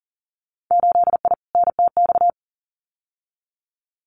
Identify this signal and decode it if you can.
Morse code, heard as a keyed tone, about 41 wpm, 700 Hz. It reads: 8I NTX
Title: Winter Field Day Exchanges Encore 40wpm